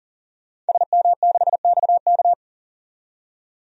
Morse code, heard as a keyed tone, about 40 wpm, 700 Hz. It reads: SM6XK
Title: DXCC-100 Call Signs - 5 Characters - 40wpm